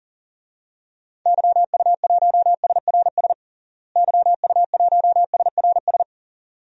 Morse code, heard as a keyed tone, about 40 wpm, 700 Hz. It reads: YU1SRS YU1SRS
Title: DXCC-100 Call Signs - 6 Characters - Repeated 2x - 40wpm